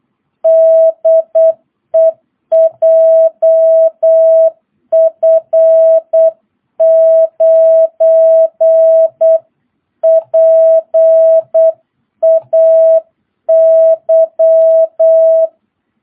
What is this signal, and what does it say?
Morse code, heard as a keyed tone, about 8 wpm, 650 Hz. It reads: DE JF9PAY
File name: jh8cbh / 2022cw